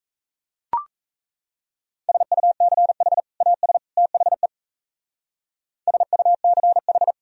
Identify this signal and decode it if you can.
Morse code, heard as a keyed tone, about 37 wpm, 700 Hz. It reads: SUCH AS THE SUCH